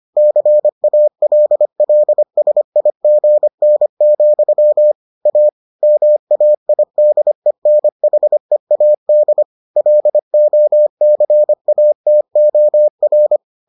Morse code, heard as a keyed tone, about 25 wpm, 600 Hz. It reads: CALLSIGN, A MAIDENHEAD LOCATOR